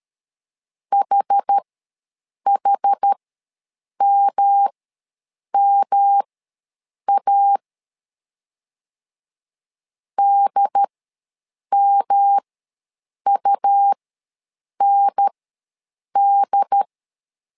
Morse code, heard as a keyed tone, about 13 wpm, spaced as for 7 wpm, 800 Hz. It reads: HHMMA DMUND
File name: LEZIONI / 3 MWRO / 3 M ws 12wpm 1745